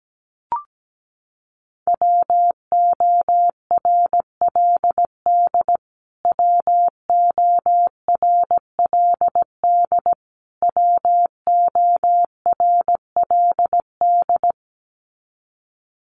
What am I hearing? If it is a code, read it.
Morse code, heard as a keyed tone, about 17 wpm, 700 Hz. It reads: WORLD WORLD WORLD
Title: Top 300 Words Repeated 17wpm